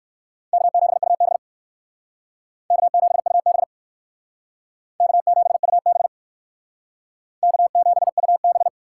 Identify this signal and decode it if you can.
Morse code, heard as a keyed tone, about 66 wpm, 700 Hz. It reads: K7UB K7UB K7UB K7UB